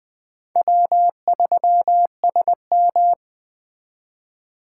Morse code, heard as a keyed tone, about 21 wpm, 700 Hz. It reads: W3SM